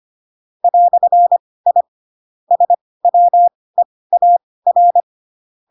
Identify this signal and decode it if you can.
Morse code, heard as a keyed tone, about 25 wpm, spaced as for 18 wpm, 700 Hz. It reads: "I SWEAR
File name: Pinocchio / Pinocchio ch2 12 25 ik1hge